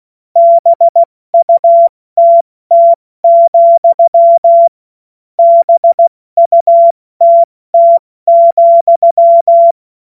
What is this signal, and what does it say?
Morse code, heard as a keyed tone, about 16 wpm, 680 Hz. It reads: BUTT, BUTT,
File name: Morsekurs / Witze / Witz 08. August 2023 70 F80